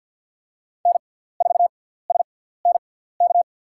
Morse code, heard as a keyed tone, about 54 wpm, spaced as for 17 wpm, 700 Hz. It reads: N4SNK
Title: USA Call Signs - 5 Characters - 6x Character Spacing - 50wpm